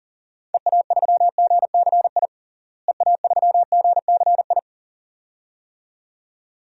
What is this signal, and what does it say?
Morse code, heard as a keyed tone, about 40 wpm, 700 Hz. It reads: EA3GCI EA3GCI
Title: DXCC-100 Call Signs - 6 Characters - Repeated 2x - 40wpm